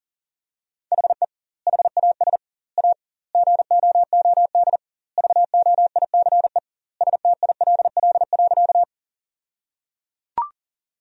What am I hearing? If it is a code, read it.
Morse code, heard as a keyed tone, about 37 wpm, 700 Hz. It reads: HE HAS A GOOD VOICE STILL.
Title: 2x Word-Spacing - Sentences from Top 500 Words - 40wpm